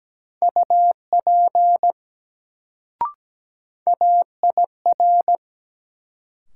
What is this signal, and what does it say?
Morse code, heard as a keyed tone, about 17 wpm, 700 Hz. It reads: UP AIR
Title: Sets of 2 Words - New Word Drill - 17wpm